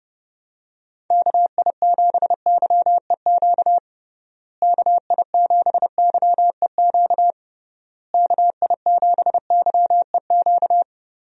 Morse code, heard as a keyed tone, about 30 wpm, 700 Hz. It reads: KI7YEQ KI7YEQ KI7YEQ
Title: USA Call Signs - 6 Characters - Repeated 3x - 30wpm